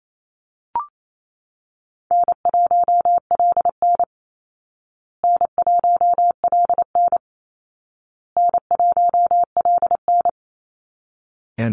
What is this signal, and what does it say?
Morse code, heard as a keyed tone, about 28 wpm, 700 Hz. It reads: N1LN N1LN N1LN
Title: Call Signs Repeated 2x - 28wpm